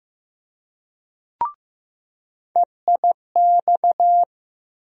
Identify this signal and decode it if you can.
Morse code, heard as a keyed tone, about 15 wpm, 700 Hz. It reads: EIX
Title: USA Call Sign Postfixes - 3 Letters 15wpm